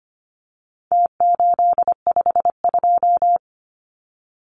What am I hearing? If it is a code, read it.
Morse code, heard as a keyed tone, about 25 wpm, 700 Hz. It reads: T852
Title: Numbers - Contest Style - 4-Digit 25wpm